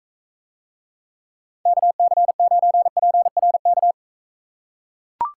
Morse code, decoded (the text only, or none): KC9PRK